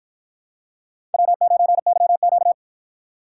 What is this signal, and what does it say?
Morse code, heard as a keyed tone, about 49 wpm, 700 Hz. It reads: W9YQ